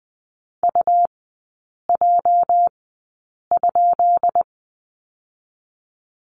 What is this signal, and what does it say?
Morse code, heard as a keyed tone, about 19 wpm, 700 Hz. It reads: U J ?
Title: 205 Lesson 33e - Letter j - Three Characters 20wpm